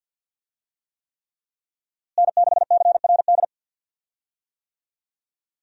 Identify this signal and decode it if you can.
Morse code, heard as a keyed tone, about 52 wpm, 700 Hz. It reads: N6CRD
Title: USA Call Signs - 5 Characters - Repeated 3x - 50wpm